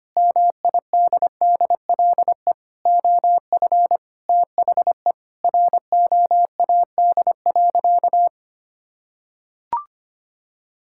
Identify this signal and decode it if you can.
Morse code, heard as a keyed tone, about 25 wpm, 700 Hz. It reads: MIDDLE OF THE ROAD.
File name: practice / 25wpm / Sentences with 3-Word Phrases - 25wpm